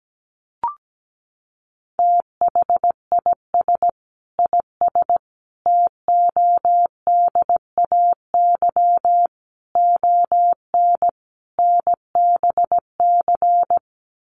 Morse code, heard as a keyed tone, about 17 wpm, 700 Hz. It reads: THIS IS TODAY ON NBC
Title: Sets of 5 Words Encore 17wpm